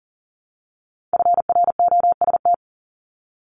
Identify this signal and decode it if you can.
Morse code, heard as a keyed tone, about 40 wpm, 700 Hz. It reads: FROST
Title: Multi-letter Phonemes Repeated 3x 40wpm